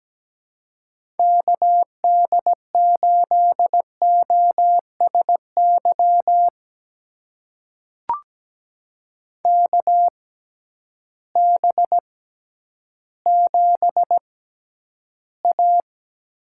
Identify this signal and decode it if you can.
Morse code, heard as a keyed tone, about 17 wpm, spaced as for 10 wpm, 700 Hz. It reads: KD8OSY KB7A
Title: USA Call Signs - 6 Characters - 6x Character Spacing - 17wpm